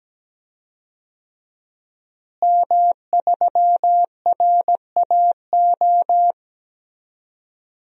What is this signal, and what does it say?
Morse code, heard as a keyed tone, about 17 wpm, 700 Hz. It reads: M3RAO
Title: DXCC-100 Call Signs - 5 Characters - Repeated 3x - 17wpm